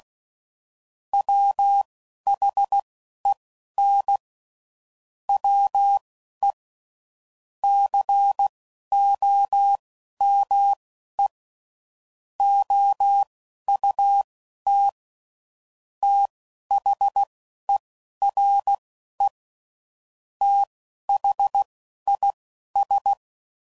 Morse code, heard as a keyed tone, about 16 wpm, spaced as for 11 wpm, 800 Hz. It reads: WHEN WE COME OUT THERE THIS
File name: CW Code Groups @10WPM / QSOs / QSO WD0FBT